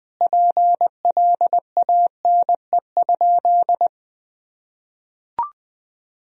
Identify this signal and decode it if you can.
Morse code, heard as a keyed tone, about 20 wpm, 700 Hz. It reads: PLANE?
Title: Sentences from Top 1000 Words - Review Part 3 - 20wpm